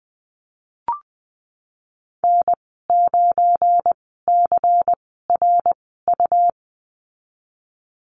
Morse code, heard as a keyed tone, about 20 wpm, spaced as for 15 wpm, 700 Hz. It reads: N9CRU
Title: 186 Lesson 30g - Letter b - Call Signs 20wpm